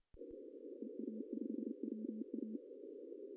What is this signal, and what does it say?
Morse code, heard as a keyed tone, about 28 wpm, 230 Hz. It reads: EA5WA
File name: CW / cqww-cw-k5tr